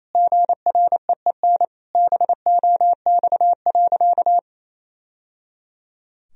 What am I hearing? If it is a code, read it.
Morse code, heard as a keyed tone, about 27 wpm, 700 Hz. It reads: GREEN BOX.
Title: Sentences from Top 1000 Words - Review Part 2 - 28wpm